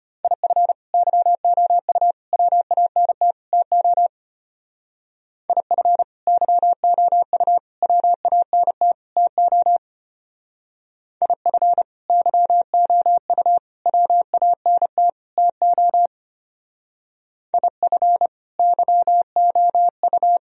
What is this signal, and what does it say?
Morse code, heard as a keyed tone, about 30 wpm, 700 Hz. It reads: IF YOU WANT TO IF YOU WANT TO IF YOU WANT TO IF YOU
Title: Speed-Racing - Sets of 4 Words 25wpm